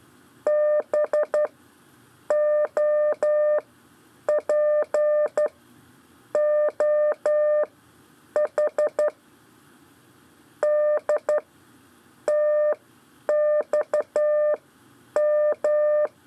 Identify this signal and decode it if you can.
Morse code, heard as a keyed tone, about 12 wpm, spaced as for 8 wpm, 600 Hz. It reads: BOPOH DTXM
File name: 4 LEZIONE CW 1